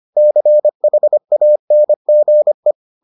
Morse code, heard as a keyed tone, about 25 wpm, 600 Hz. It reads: CHANGE